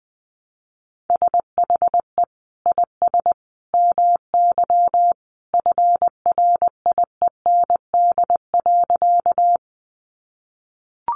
Morse code, heard as a keyed tone, about 20 wpm, 700 Hz. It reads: SHE IS MY FRIEND.
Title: Sentences from Top 300 Words 20wpm